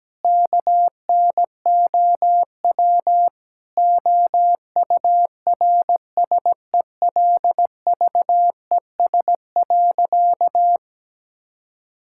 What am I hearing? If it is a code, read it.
Morse code, heard as a keyed tone, about 17 wpm, 700 Hz. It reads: KNOW OURSELVES.
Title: Sentences from Top 1600 Words 17wpm